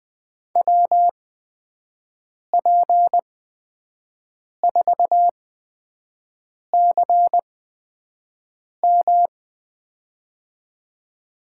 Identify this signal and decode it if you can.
Morse code, heard as a keyed tone, about 20 wpm, spaced as for 5 wpm, 700 Hz. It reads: WP4CM
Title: USA Call Signs - 8x Character Spacing - 20wpm